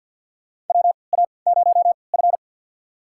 Morse code, heard as a keyed tone, about 51 wpm, spaced as for 28 wpm, 700 Hz. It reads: WA0F
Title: USA Call Signs - 3x Character Spacing - 50wpm